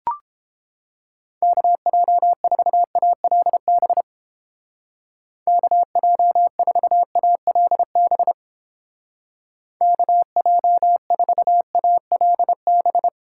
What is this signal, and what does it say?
Morse code, heard as a keyed tone, about 31 wpm, 700 Hz. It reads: KJ4ALB KJ4ALB KJ4ALB